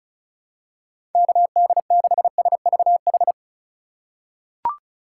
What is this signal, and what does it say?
Morse code, heard as a keyed tone, about 36 wpm, 700 Hz. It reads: KD6SVH
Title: USA Call Signs - 2x Character Spacing - 35wpm